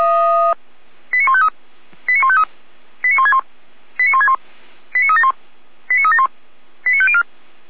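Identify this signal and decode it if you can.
Five-tone selective call (ZVEI), heard as quick runs of five tones, tones of about 70 ms. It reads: 89153 89153 89161 89161 89371 89371 89504
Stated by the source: ZVEI five-tone selective call 89099 received on SDR, archived by feuerwehr-orlamuende fire brigade server.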